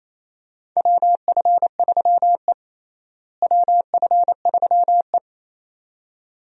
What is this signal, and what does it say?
Morse code, heard as a keyed tone, about 28 wpm, 700 Hz. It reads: WF3E WF3E
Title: USA Call Signs - 4 Characters - Repeated 2x - 28wpm